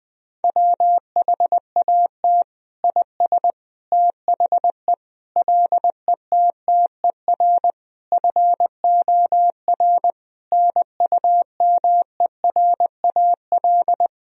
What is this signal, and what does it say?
Morse code, heard as a keyed tone, about 20 wpm, 700 Hz. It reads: WHAT IS THE LETTER FOR NUMERAL
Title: Sentences from Top 1000 Words - Review Part 2 - 20wpm